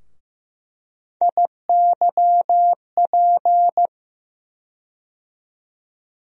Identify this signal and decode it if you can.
Morse code, heard as a keyed tone, about 16 wpm, 700 Hz. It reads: IYP
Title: USA Call Sign Postfixes - 3 Letters 15wpm